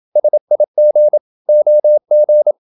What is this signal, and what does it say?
Morse code, heard as a keyed tone, about 28 wpm, 600 Hz. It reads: SIG OG
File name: Ferie QRQ 27wpm